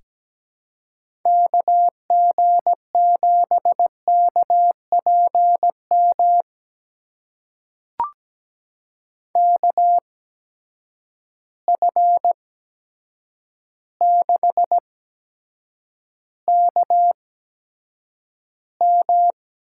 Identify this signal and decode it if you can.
Morse code, heard as a keyed tone, about 17 wpm, spaced as for 8 wpm, 700 Hz. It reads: KG7KPM KF6KM
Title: USA Call Signs - 6 Characters - 8x Character Spacing - 17wpm